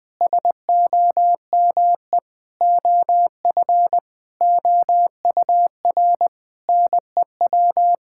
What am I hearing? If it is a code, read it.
Morse code, heard as a keyed tone, about 21 wpm, 700 Hz. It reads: SOME OF OUR NEW
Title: Sentences with 9-Word Phrases - 20wpm